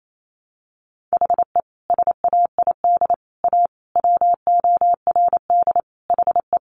Morse code, heard as a keyed tone, about 28 wpm, 700 Hz. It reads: HE SAID A WORD HE
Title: Sentences from Top 600 Words 28wpm